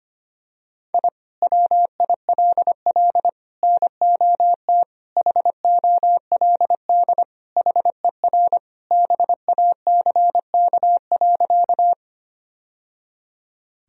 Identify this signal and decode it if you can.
Morse code, heard as a keyed tone, about 25 wpm, 700 Hz. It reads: I WILL NOT HOLD HER BACK.
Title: Sentences from Top 1000 Words - Review Part 2 - 25wpm